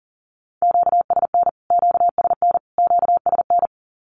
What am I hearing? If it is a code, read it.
Morse code, heard as a keyed tone, about 40 wpm, 700 Hz. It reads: QSN QSN QSN
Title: Q-Codes Repeated 3x 40wpm